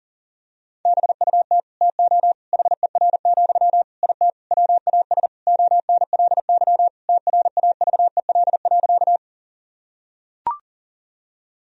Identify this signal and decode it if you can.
Morse code, heard as a keyed tone, about 40 wpm, 700 Hz. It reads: BUT TO HER, IT WAS ONLY TRAVEL.
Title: Sentences from Top 1000 Words - Review Part 2 - 40wpm